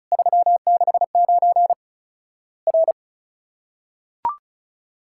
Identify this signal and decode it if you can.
Morse code, heard as a keyed tone, about 36 wpm, 700 Hz. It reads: 369 R
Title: Worked All Europe DX Contest - QTC Exchange - 35wpm